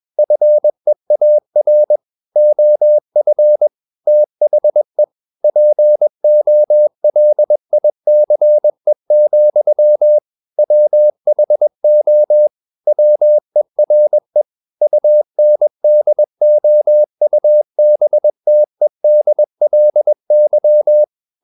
Morse code, heard as a keyed tone, about 21 wpm, 600 Hz. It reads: FEAR OF THE POLICE, WHO WERE UNDOUBTEDLY
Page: CW med Gnister